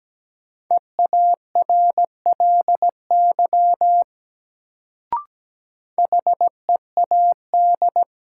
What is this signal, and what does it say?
Morse code, heard as a keyed tone, about 17 wpm, 700 Hz. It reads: EARLY HEAD
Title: Sets of 3 Words - New Word Drill - 17wpm